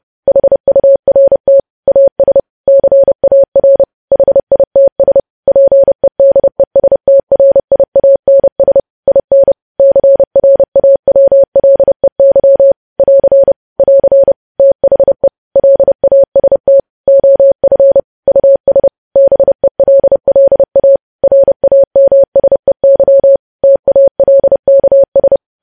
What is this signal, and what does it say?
Morse code, heard as a keyed tone, about 30 wpm, 550 Hz. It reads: HURT AS CAR HITS PEDESTRIANS IN CRAWLEY <AR> <AR> THE LAST OF US BELLA RAMSEY TALKS
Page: news headlines in morse code at 30 wpm